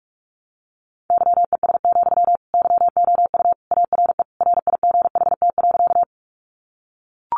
Morse code, heard as a keyed tone, about 44 wpm, 700 Hz. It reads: YES, YOU ARE RIGHT.
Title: Sentences from Top 500 Words 45wpm